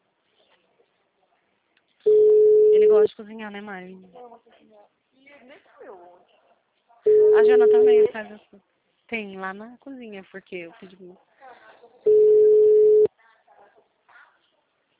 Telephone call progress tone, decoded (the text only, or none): ringback tone